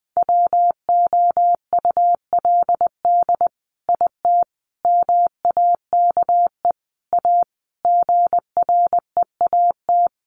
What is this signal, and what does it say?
Morse code, heard as a keyed tone, about 20 wpm, 700 Hz. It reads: WOULD IT MAKE A GREAT
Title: Sentences from Top 800 Words 20wpm